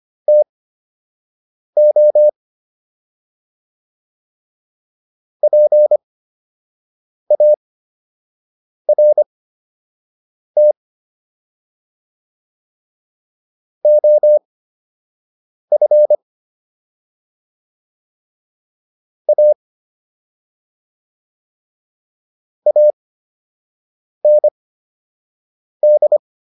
Morse code, heard as a keyed tone, about 25 wpm, spaced as for 6 wpm, 600 Hz. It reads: TO PART OF A AND